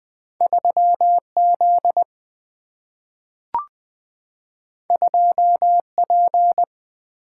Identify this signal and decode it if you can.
Morse code, heard as a keyed tone, about 20 wpm, 700 Hz. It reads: 3Z 2P
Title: Single-Letter Single-Number 20wpm